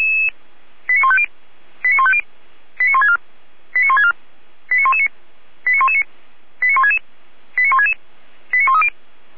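Five-tone selective call (ZVEI), heard as quick runs of five tones, tones of about 70 ms. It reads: 89150 89150 89164 89164 89119 89119 89166 89166 89130
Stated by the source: ZVEI five-tone selective call, code 89080, received with an SDR.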